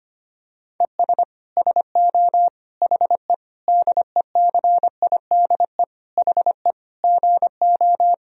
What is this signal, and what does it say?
Morse code, heard as a keyed tone, about 25 wpm, 700 Hz. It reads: ES SO HE DECIDE HE GO
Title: Sentences from Top 500 Words and Abbreviations 25wpm